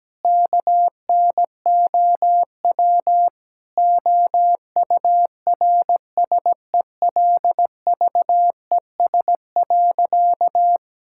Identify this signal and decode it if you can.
Morse code, heard as a keyed tone, about 17 wpm, 700 Hz. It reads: KNOW OURSELVES.
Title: Sentences from Top 1600 Words 17wpm